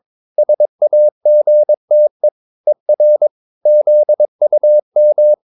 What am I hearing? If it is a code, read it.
Morse code, heard as a keyed tone, about 22 wpm, 600 Hz. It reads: SAGTE ER ZUM